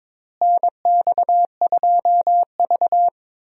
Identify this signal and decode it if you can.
Morse code, heard as a keyed tone, about 22 wpm, 700 Hz. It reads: NX2V
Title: USA Call Signs - 4 Characters - Repeated 3x - 22wpm